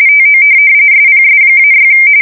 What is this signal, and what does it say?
RTTY, about 45.45 baud. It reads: TO SPS-6
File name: fsk170 / USS-New-Jersey